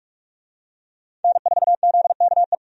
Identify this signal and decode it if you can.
Morse code, heard as a keyed tone, about 45 wpm, 700 Hz. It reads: N4ZKE
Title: USA Call Signs - 5 Characters - 45wpm